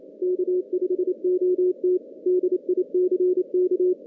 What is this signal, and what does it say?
Morse code, heard as a keyed tone, about 27 wpm, 410 Hz. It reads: K5OT DICK
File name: 2016 JAN CW NAQP / 2016 Jan naqp k5ot - 0-8